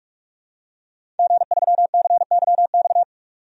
Morse code, heard as a keyed tone, about 46 wpm, 700 Hz. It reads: G3CYX
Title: DXCC-100 Call Signs - 5 Characters - 45wpm